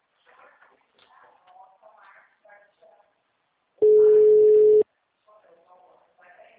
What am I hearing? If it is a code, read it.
Ringback tone (Europe, 425 Hz), with speech.